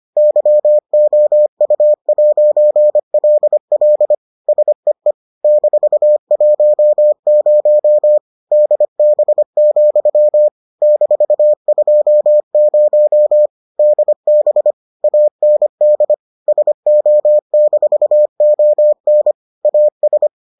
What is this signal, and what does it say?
Morse code, heard as a keyed tone, about 25 wpm, 600 Hz. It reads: YOU'LL SEE -10 DB, -20 DB AND SO-ON AS